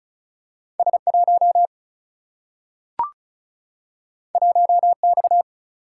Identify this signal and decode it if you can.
Morse code, heard as a keyed tone, about 35 wpm, 700 Hz. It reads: S1 1X
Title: Single-Letter Single-Number 35wpm